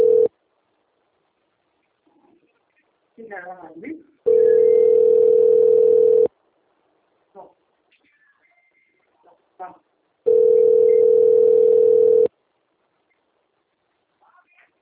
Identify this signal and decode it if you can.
Ringback tone (North America), 440 + 480 Hz, with speech.
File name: out- - - - - .2476